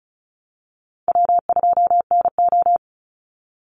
Morse code, heard as a keyed tone, about 34 wpm, 700 Hz. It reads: W2NO
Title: Easy Call Signs Repeated 3x - 35wpm